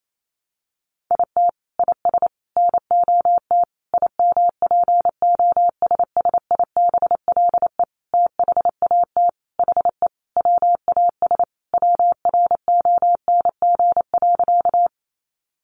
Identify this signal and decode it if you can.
Morse code, heard as a keyed tone, about 28 wpm, 700 Hz. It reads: IT IS NOT IMPOSSIBLE THAT HE WAS WRONG.
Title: Sentences from Top 1100 Words 28wpm